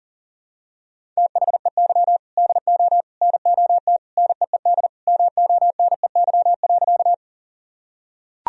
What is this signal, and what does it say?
Morse code, heard as a keyed tone, about 40 wpm, 700 Hz. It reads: THEY DO NOT NEED MONEY.
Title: Sentences from Top 1000 Words - Review Part 1 - 40wpm